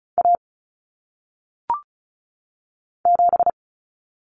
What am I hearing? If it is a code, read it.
Morse code, heard as a keyed tone, about 36 wpm, 700 Hz. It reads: A 7
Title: Common Cut Numbers - 1-Digit 35wpm